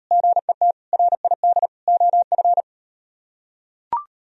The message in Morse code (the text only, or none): GET RID OF